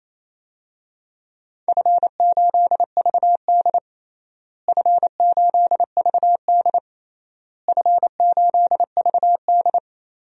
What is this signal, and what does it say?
Morse code, heard as a keyed tone, about 28 wpm, 700 Hz. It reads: F8VD F8VD F8VD
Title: DXCC-100 Call Signs - 4 Characters - Repeated 3x - 28wpm